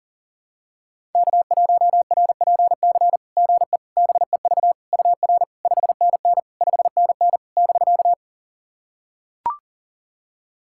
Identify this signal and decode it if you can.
Morse code, heard as a keyed tone, about 40 wpm, 700 Hz. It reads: K1RPC GE BEV UR 5NN 5NN <BK>
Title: SOTA QSO Practice 40wpm